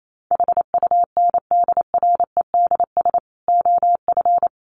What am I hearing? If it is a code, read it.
Morse code, heard as a keyed tone, about 28 wpm, 700 Hz. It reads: HUNDREDS OF
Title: Sets of 5 Words Encore2 28wpm